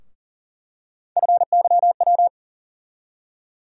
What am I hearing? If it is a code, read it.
Morse code, heard as a keyed tone, about 40 wpm, 700 Hz. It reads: FYW
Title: USA Call Sign Postfixes - 3 Letters 40wpm